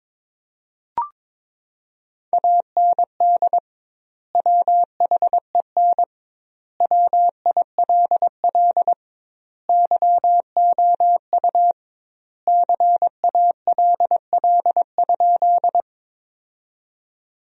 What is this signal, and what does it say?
Morse code, heard as a keyed tone, about 22 wpm, 700 Hz. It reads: AND WHEN WILL YOU CALL?
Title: 2x Word-Spacing - Sentences from Top 100 Words - 22wpm